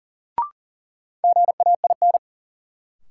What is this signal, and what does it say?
Morse code, heard as a keyed tone, about 40 wpm, 700 Hz. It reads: GAIN
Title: Sentences from Top 1000 Words - New Word Drill - 40wpm